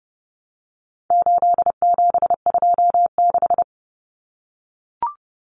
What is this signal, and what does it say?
Morse code, heard as a keyed tone, about 30 wpm, 700 Hz. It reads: 8726
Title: Common Cut Numbers - 4-Digit 30wpm